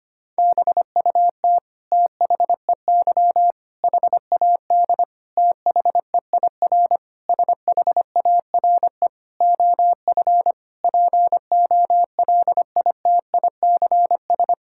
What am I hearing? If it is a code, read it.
Morse code, heard as a keyed tone, about 25 wpm, 700 Hz. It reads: BUT THEY HAD THEIR SHARE OF POLITICS
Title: Sentences from Top 1000 Words - Review Part 6 - 25wpm